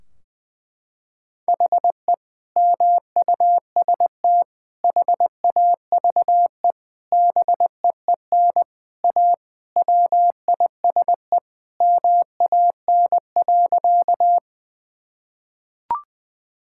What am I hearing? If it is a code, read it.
Morse code, heard as a keyed tone, about 21 wpm, 700 Hz. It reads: HE MUST HAVE BEEN A WISE MAN.
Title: Sentences from Top 1600 Words 20wpm